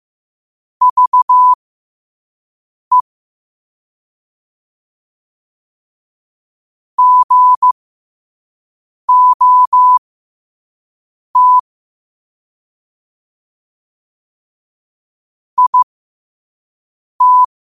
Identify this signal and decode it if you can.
Morse code, heard as a keyed tone, about 15 wpm, spaced as for 5 wpm, 1000 Hz. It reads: VE GOT IT
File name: qotdfiles / 5wpm